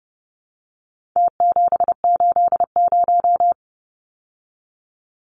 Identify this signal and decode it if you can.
Morse code, heard as a keyed tone, about 30 wpm, 700 Hz. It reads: T780
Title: Numbers - Contest Style - 4-Digit 30wpm